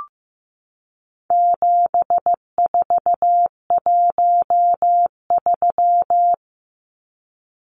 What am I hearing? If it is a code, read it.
Morse code, heard as a keyed tone, about 15 wpm, 700 Hz. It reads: 7413
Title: Numbers - 4-Digit 15wpm